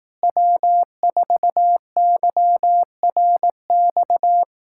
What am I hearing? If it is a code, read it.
Morse code, heard as a keyed tone, about 18 wpm, 700 Hz. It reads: W4YRX
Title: Speed-Racing Common QSO Phrases Encore - 15wpm